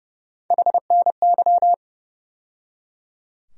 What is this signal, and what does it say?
Morse code, heard as a keyed tone, about 30 wpm, 700 Hz. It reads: HNY
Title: USA Call Sign Postfixes - 3 Letters 30wpm